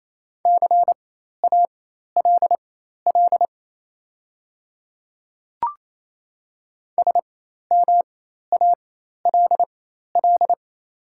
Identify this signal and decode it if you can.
Morse code, heard as a keyed tone, about 28 wpm, spaced as for 13 wpm, 700 Hz. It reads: CALL SMALL
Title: 4x Character Spacing - Top 200 Words - 28wpm